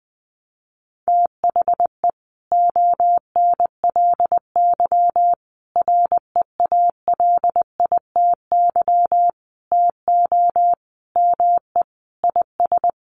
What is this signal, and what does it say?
Morse code, heard as a keyed tone, about 20 wpm, 700 Hz. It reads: THE ONLY REALITY TO ME IS
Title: Sentences from Top 1200 Words 20wpm